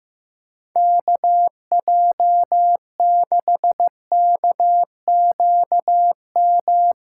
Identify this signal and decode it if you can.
Morse code, heard as a keyed tone, about 16 wpm, 700 Hz. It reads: KJ6KQM
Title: Practice QSO - Exchange 4 - Ending - 15wpm